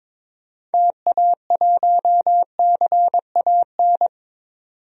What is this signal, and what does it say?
Morse code, heard as a keyed tone, about 22 wpm, 700 Hz. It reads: TA1CAN
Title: DXCC-100 Call Signs - 6 Characters - 22wpm